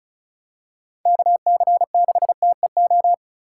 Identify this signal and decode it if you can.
Morse code, heard as a keyed tone, about 36 wpm, 700 Hz. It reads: KC6TEO
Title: USA Call Signs - 4x Character Spacing - 35wpm